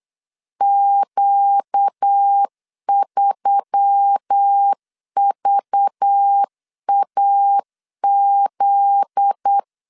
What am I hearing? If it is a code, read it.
Morse code, heard as a keyed tone, about 8 wpm, 800 Hz. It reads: Q3VAZ
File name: LEZIONI / 7 GF9Z / 7 GF9Z 8wpm 3213